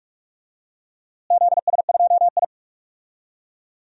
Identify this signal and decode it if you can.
Morse code, heard as a keyed tone, about 46 wpm, 700 Hz. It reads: ZS2I
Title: DXCC-100 Call Signs - 4 Characters - Repeated 2x - 45wpm